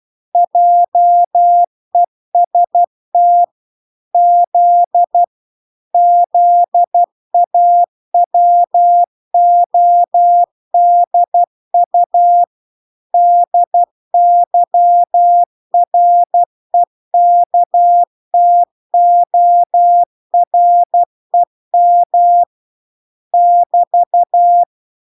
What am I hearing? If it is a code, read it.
Morse code, heard as a keyed tone, about 12 wpm, 700 Hz. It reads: JEST Z ZAWODU DYREKTOREM <BT>